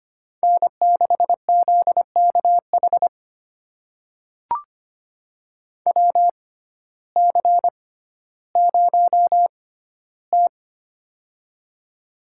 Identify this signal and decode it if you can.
Morse code, heard as a keyed tone, about 25 wpm, spaced as for 14 wpm, 700 Hz. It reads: N6ZKH WC0T
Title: USA Call Signs - 6x Character Spacing - 25wpm